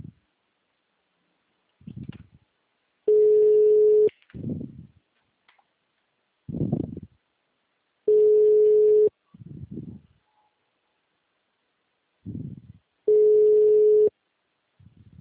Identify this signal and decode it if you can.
Ringback tone (Europe, 425 Hz).
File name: out- -2011- - - .773